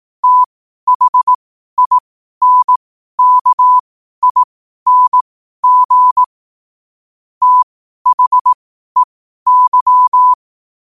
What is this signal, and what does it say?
Morse code, heard as a keyed tone, about 18 wpm, spaced as for 12 wpm, 1000 Hz. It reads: THINKING THEY
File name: qotdfiles / 12wpm